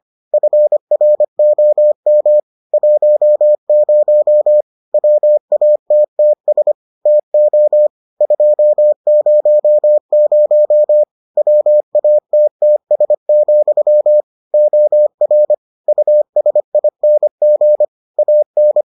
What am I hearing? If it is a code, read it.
Morse code, heard as a keyed tone, about 25 wpm, 600 Hz. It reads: FROM 10 WATTS TO 200 WATTS, OR USING AN